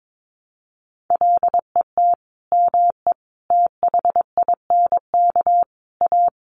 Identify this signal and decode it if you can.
Morse code, heard as a keyed tone, about 22 wpm, 700 Hz. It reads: LET ME THINK A
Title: Sentences from Top 1000 Words 22wpm